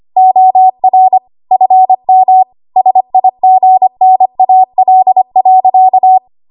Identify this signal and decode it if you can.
Morse code, heard as a keyed tone, about 25 wpm, 750 Hz. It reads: OR FM SIGNAL.
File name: Archive / 25 WPM / 25WPM